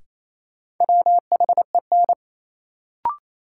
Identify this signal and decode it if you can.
Morse code, heard as a keyed tone, about 28 wpm, 700 Hz. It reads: WHEN
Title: Sentences from Top 100 Words - New Word Drill - 28wpm